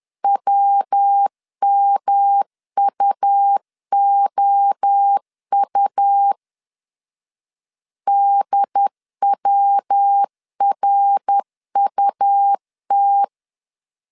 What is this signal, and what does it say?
Morse code, heard as a keyed tone, about 10 wpm, 800 Hz. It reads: WMUOU DWRUT
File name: CW / DET / 1 3 1500C S B 10wpm 3335